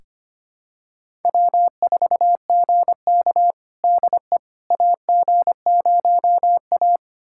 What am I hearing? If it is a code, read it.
Morse code, heard as a keyed tone, about 25 wpm, 700 Hz. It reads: W4GK DE AG0A
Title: USA Callsign DE Callsign - 4 Characters - 25wpm